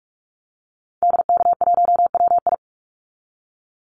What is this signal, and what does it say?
Morse code, heard as a keyed tone, about 44 wpm, 700 Hz. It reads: DK1WI
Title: Call Signs - 45wpm